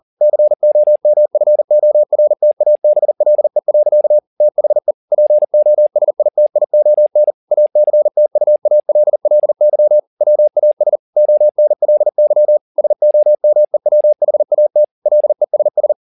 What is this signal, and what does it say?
Morse code, heard as a keyed tone, about 40 wpm, 600 Hz. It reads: COMFORTABLE. THE POSITION ACTUALLY WAS ONLY SOMEWHAT LESS